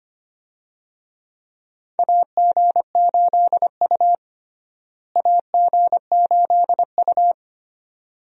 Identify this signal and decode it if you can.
Morse code, heard as a keyed tone, about 25 wpm, 700 Hz. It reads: AG8U AG8U
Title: USA Call Signs - 4 Characters - Repeated 3x - 25wpm